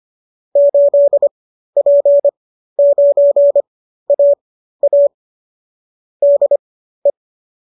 Morse code, heard as a keyed tone, about 25 wpm, spaced as for 13 wpm, 570 Hz. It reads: 8P9AA DE